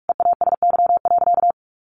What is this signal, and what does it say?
Morse code, heard as a keyed tone, about 44 wpm, 700 Hz. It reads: EASY.
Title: Sentences from Top 700 Words 45wpm